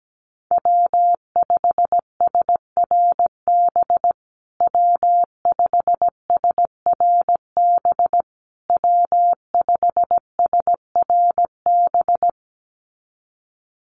Morse code, heard as a keyed tone, about 17 wpm, 700 Hz. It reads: W5SRB W5SRB W5SRB
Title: USA Call Signs Repeated 3x 17wpm